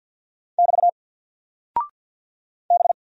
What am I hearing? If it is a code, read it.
Morse code, heard as a keyed tone, about 48 wpm, 700 Hz. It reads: <BT> B
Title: Single Character (Letter:Number:Punctuation:Prosign) Alt 50wpm